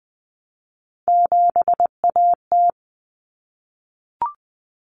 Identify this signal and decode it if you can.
Morse code, heard as a keyed tone, about 20 wpm, 700 Hz. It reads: 7AT
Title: Common Cut Numbers - 3-Digit 20wpm